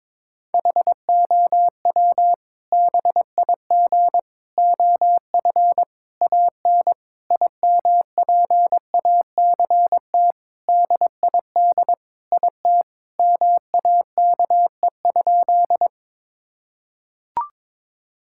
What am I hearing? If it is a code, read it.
Morse code, heard as a keyed tone, about 21 wpm, 700 Hz. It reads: HOW BIG OF AN IMPACT DID IT MAKE?
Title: Sentences from Top 1000 Words - Review Part 6 - 22wpm